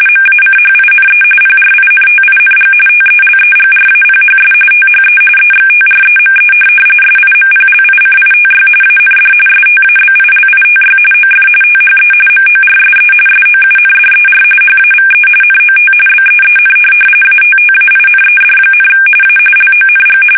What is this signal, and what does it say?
RTTY, about 45.45 baud. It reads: MACHINE PRINTING TELEGRAPHS THE EXTREME SIMPLICITY AND STABILITY OF THE MORSE SOUNDER HAVE NOT BEEN ATTAINED, A LONG